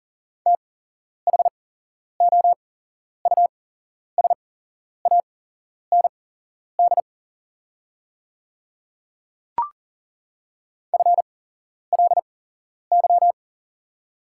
Morse code, heard as a keyed tone, about 40 wpm, spaced as for 11 wpm, 700 Hz. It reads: THOUSAND FLY
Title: 8x Character Spacing - Top 500 Words - 40wpm